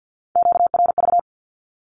Morse code, heard as a keyed tone, about 50 wpm, 700 Hz. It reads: QRV
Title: Q-Codes 50wpm